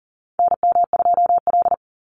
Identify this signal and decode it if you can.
Morse code, heard as a keyed tone, about 40 wpm, 700 Hz. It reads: NM2L
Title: Call Signs Repeated 3x - 40wpm